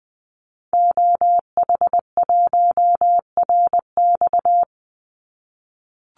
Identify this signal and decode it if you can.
Morse code, heard as a keyed tone, about 20 wpm, 700 Hz. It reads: OH1RX
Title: Call Signs - 20wpm